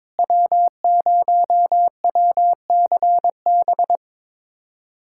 Morse code, heard as a keyed tone, about 22 wpm, 700 Hz. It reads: W0WCB
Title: USA Callsign DE Callsign 22wpm